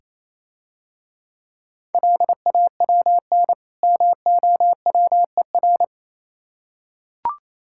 Morse code, heard as a keyed tone, about 28 wpm, 700 Hz. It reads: LAWN MOWER